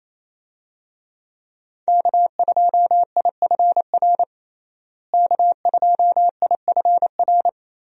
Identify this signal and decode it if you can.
Morse code, heard as a keyed tone, about 28 wpm, 700 Hz. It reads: K2IFR K2IFR
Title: USA Call Signs - 5 Characters - Repeated 3x - 28wpm